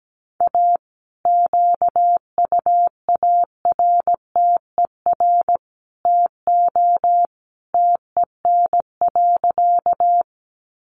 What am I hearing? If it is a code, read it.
Morse code, heard as a keyed tone, about 17 wpm, 700 Hz. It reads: A QUARTER TO TEN.
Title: Sentences from Top 1300 Words 17wpm